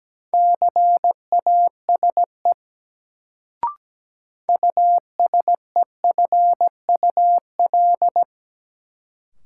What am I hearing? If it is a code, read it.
Morse code, heard as a keyed tone, about 17 wpm, 700 Hz. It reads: CASE USEFUL
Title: Sentences from Top 700 Words - New Word Drill - 17wpm